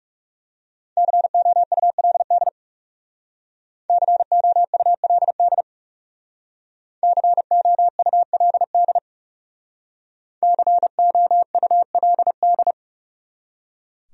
Morse code, heard as a keyed tone, about 40 wpm, 700 Hz. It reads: COULD COULD COULD COULD